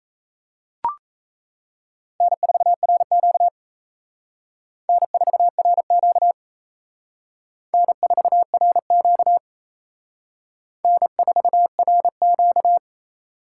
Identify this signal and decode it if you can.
Morse code, heard as a keyed tone, about 34 wpm, 700 Hz. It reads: N4RQ N4RQ N4RQ N4RQ
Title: Speed Racing - USA 1x2 Call Signs 28wpm